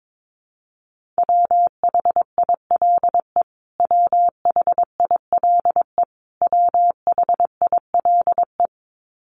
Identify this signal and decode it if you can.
Morse code, heard as a keyed tone, about 22 wpm, 700 Hz. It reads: WHILE WHILE WHILE
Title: Top 500 Words Repeated 22wpm